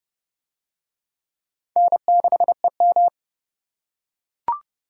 Morse code, heard as a keyed tone, about 28 wpm, 700 Hz. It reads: N6EM